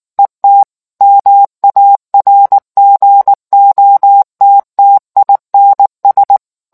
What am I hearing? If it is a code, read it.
Morse code, heard as a keyed tone, about 19 wpm, 800 Hz. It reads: ET MARGOTTINS